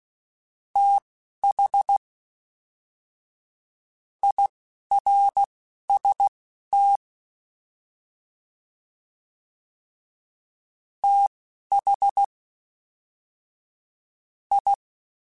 Morse code, heard as a keyed tone, about 16 wpm, spaced as for 7 wpm, 800 Hz. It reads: THIRST THI
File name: TH / TH mp3s / Fast TH @10